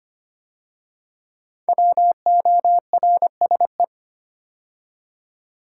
Morse code, heard as a keyed tone, about 25 wpm, 700 Hz. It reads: WORSE
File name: 5 Letter Words - First 3 Letters Given - 25wpm